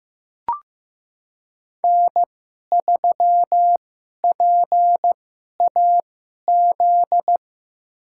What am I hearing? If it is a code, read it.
Morse code, heard as a keyed tone, about 16 wpm, spaced as for 11 wpm, 700 Hz. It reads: N3PAZ
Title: USA Call Signs - 5 Characters - 2x Character Spacing - 15wpm